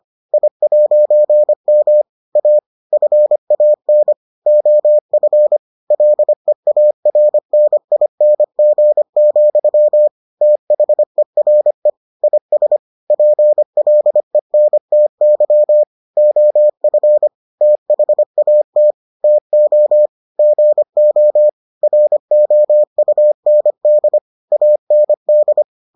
Morse code, heard as a keyed tone, about 25 wpm, 600 Hz. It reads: I'M A FAN OF LEARNING, THERE IS PLENTY OF THAT TO GO ROUND AND